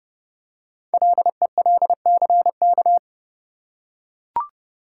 Morse code, heard as a keyed tone, about 30 wpm, 700 Hz. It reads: LELCK
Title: Code Groups - 5 Letters - 30wpm